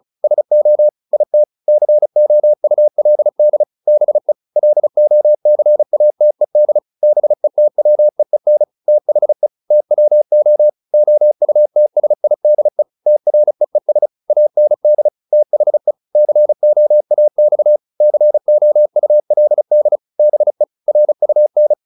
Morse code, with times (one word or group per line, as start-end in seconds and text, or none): 0.2-0.9 SO
1.1-1.4 IT
1.7-3.6 COULD
3.9-4.3 BE
4.6-6.8 LOCATED
7.0-8.6 BETWEEN
8.9-9.5 THE
9.7-10.7 TWO
10.9-12.8 OUTSIDE
13.1-14.1 TREES
14.3-15.1 AND
15.3-15.9 THE
16.1-17.8 COAX
18.0-20.0 COULD
20.2-20.6 BE
20.9-21.7 RUN